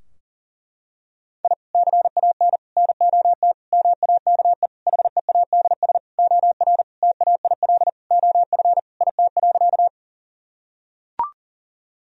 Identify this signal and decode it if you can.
Morse code, heard as a keyed tone, about 37 wpm, 700 Hz. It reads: I CAN NOT MAKE HEADS OR TAIL OF IT.
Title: Sentences from Top 1000 Words - Review Part 3 - 40wpm